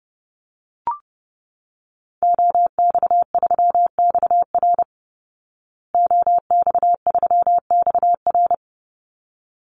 Morse code, heard as a keyed tone, about 30 wpm, 700 Hz. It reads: OX3XR OX3XR
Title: Call Signs Repeated 3x - 30wpm